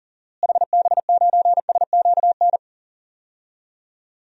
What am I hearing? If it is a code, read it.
Morse code, heard as a keyed tone, about 40 wpm, 700 Hz. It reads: HB9SQN
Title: DXCC-100 Call Signs - 6 Characters - Repeated 2x - 40wpm